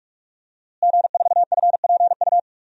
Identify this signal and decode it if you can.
Morse code, heard as a keyed tone, about 43 wpm, 700 Hz. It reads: G4FPU
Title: DXCC-100 Call Signs - 5 Characters - Repeated 2x - 45wpm